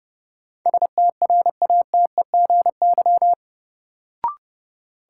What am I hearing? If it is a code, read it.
Morse code, heard as a keyed tone, about 31 wpm, 700 Hz. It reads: STRATEGY
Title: Sentences from Top 1000 Words - Review Part 5 - New Word Drill - 30wpm